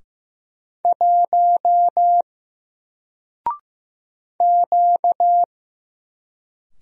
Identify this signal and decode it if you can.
Morse code, heard as a keyed tone, about 15 wpm, 700 Hz. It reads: 1 Q
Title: Single Character (Letter:Number:Punctuation:Prosign) 15wpm